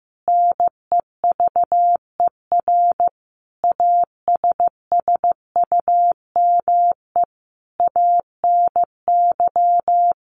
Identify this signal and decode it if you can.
Morse code, heard as a keyed tone, about 15 wpm, 700 Hz. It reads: NEVER ASSUME ANY
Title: Sentences from Top 1500 Words 15wpm